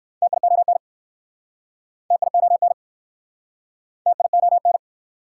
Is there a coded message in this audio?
Yes: Morse code from 0.2 to 4.8 s, heard as a keyed tone, about 59 wpm, 690 Hz. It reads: TION TION TION